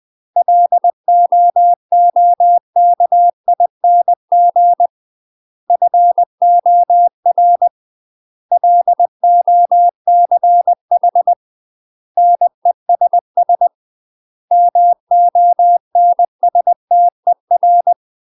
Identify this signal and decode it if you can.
Morse code, heard as a keyed tone, about 20 wpm, 700 Hz. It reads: LOOKING FOR LOCH NESS MONSTER
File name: UK MASTER QSOs TRAINING / 20WPM / UK MASTER QSOs 0072